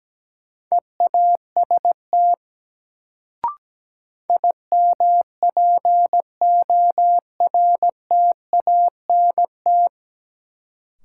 Morse code, heard as a keyed tone, about 17 wpm, 700 Hz. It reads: EAST IMPORTANT
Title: Sets of 3 Words - New Word Drill - 17wpm